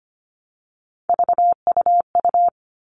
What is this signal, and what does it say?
Morse code, heard as a keyed tone, about 25 wpm, 700 Hz. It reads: VUU
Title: All Cut Numbers - 3-Digit 25wpm